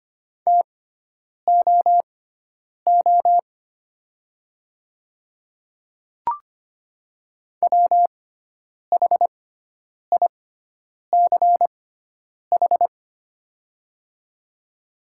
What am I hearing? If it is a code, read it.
Morse code, heard as a keyed tone, about 25 wpm, spaced as for 9 wpm, 700 Hz. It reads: TOO WHICH
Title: 6x Character Spacing - Top 200 Words - 25wpm